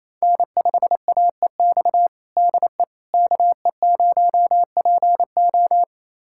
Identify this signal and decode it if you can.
Morse code, heard as a keyed tone, about 28 wpm, 700 Hz. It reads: N5AEX DE KE0PO
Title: USA Callsign DE Callsign - 5 Characters - 28wpm